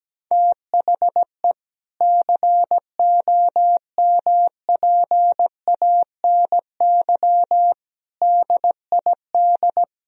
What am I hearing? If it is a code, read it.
Morse code, heard as a keyed tone, about 17 wpm, 700 Hz. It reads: THE COMPANY DID
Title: Sentences from Top 1000 Words - Review Part 3 - 17wpm